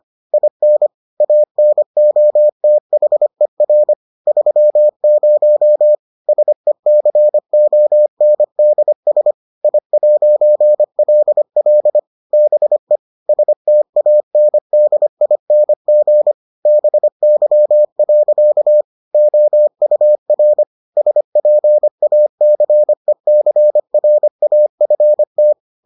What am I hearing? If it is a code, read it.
Morse code, heard as a keyed tone, about 25 wpm, 600 Hz. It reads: IN ANOTHER 30 SECONDS I'LL BE STANDING BY. OUR SPACECRAFT